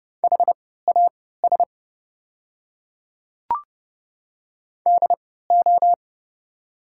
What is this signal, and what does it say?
Morse code, heard as a keyed tone, about 30 wpm, spaced as for 17 wpm, 700 Hz. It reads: HAS DO
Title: 3x Character Spacing - Top 200 Words - 30wpm